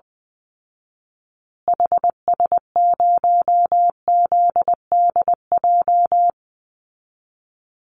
Morse code, heard as a keyed tone, about 20 wpm, 700 Hz. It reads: HS0ZDJ
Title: Call Signs Repeated 3x - 20wpm